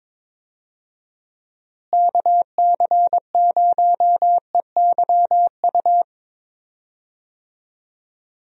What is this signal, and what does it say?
Morse code, heard as a keyed tone, about 22 wpm, 700 Hz. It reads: KC0EYU
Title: USA Call Signs - 6 Characters - Repeated 3x - 22wpm